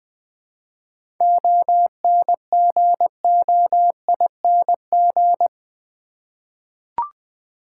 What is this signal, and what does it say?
Morse code, heard as a keyed tone, about 20 wpm, 690 Hz. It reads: ONGOING
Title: 7 Letter Words - First 3 Letters Given - 20wpm